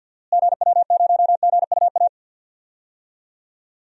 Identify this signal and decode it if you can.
Morse code, heard as a keyed tone, about 50 wpm, 700 Hz. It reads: GW0GUA
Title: DXCC-100 Call Signs - 6 Characters - Repeated 2x - 50wpm